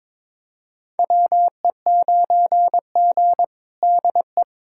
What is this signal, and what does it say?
Morse code, heard as a keyed tone, about 22 wpm, 700 Hz. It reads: WE9G DE
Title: USA Callsign DE Callsign - 4 Characters - 22wpm